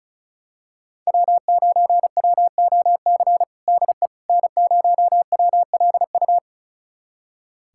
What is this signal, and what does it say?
Morse code, heard as a keyed tone, about 34 wpm, 700 Hz. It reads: W9WOC DE N0WLU
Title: USA Callsign DE Callsign - 5 Characters - 35wpm